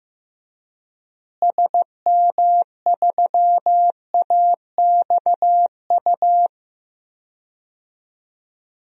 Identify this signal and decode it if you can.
Morse code, heard as a keyed tone, about 15 wpm, 700 Hz. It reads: SM3AXU
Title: DXCC-100 Call Signs - 6 Characters - Repeated 3x - 15wpm